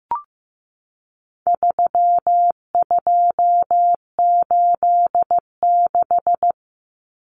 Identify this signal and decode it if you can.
Morse code, heard as a keyed tone, about 15 wpm, 700 Hz. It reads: 3286
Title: Common Cut Numbers - 4-Digit 15wpm